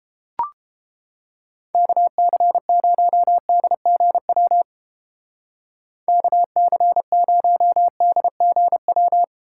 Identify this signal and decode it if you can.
Morse code, heard as a keyed tone, about 29 wpm, 700 Hz. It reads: KC0DGW KC0DGW